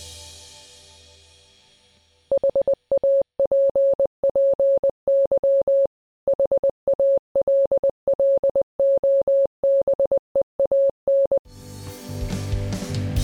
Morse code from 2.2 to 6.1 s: HAPPY